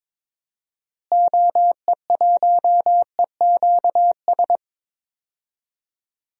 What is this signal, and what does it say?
Morse code, heard as a keyed tone, about 21 wpm, 700 Hz. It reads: OE1EQS
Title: DXCC-100 Call Signs - 6 Characters - Repeated 3x - 22wpm